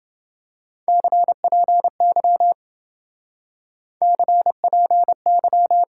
Morse code, heard as a keyed tone, about 28 wpm, 700 Hz. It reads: CPY CPY